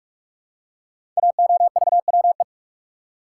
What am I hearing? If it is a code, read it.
Morse code, heard as a keyed tone, about 44 wpm, 700 Hz. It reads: AOVWE
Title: Code Groups - 5 Letters - 45wpm